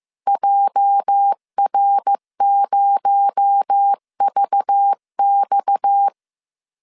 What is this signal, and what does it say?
Morse code, heard as a keyed tone, about 15 wpm, 800 Hz. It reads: JR0VX
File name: CW / DET / 1 9 1500C S C 14wpm 3245